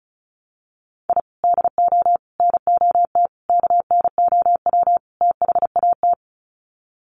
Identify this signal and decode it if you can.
Morse code, heard as a keyed tone, about 34 wpm, 700 Hz. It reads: I DO NOT KNOW THAT
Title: Sets of 5 Words 35wpm